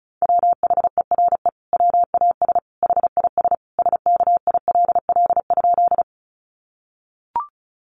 Morse code, heard as a keyed tone, about 34 wpm, 700 Hz. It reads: WHERE WAS HIS SKILL?
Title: Sentences from Top 1200 Words 35wpm